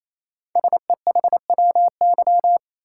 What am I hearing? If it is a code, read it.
Morse code, heard as a keyed tone, about 29 wpm, 700 Hz. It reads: SEHWY
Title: Code Groups - 5 Letters - 28wpm